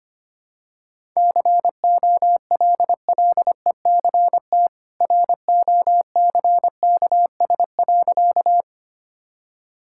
Morse code, heard as a keyed tone, about 25 wpm, 700 Hz. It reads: COLLECT ROCKS.
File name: Easy Sentences with Hints - 25wpm